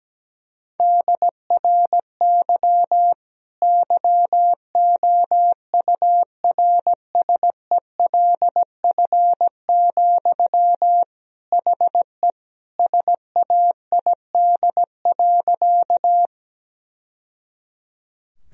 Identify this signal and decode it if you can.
Morse code, heard as a keyed tone, about 17 wpm, 700 Hz. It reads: DRY YOURSELF, HE SAID.
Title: Sentences from Top 1000 Words - Review Part 3 - 17wpm